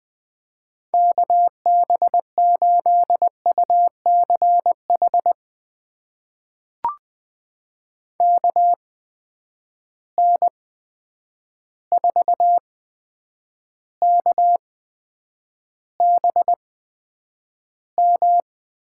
Morse code, heard as a keyed tone, about 20 wpm, spaced as for 9 wpm, 700 Hz. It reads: KB8UCH KN4KBM
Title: USA Call Signs - 6 Characters - 8x Character Spacing - 20wpm